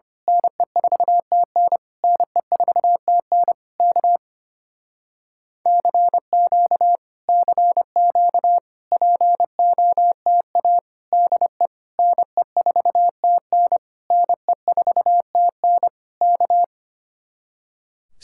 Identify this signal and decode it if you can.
Morse code, heard as a keyed tone, about 26 wpm, 700 Hz. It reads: NE4TN NE4TN K CQ CQ POTA DE NE4TN NE4TN K
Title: Speed-Racing Common QSO Phrases Encore - 25wpm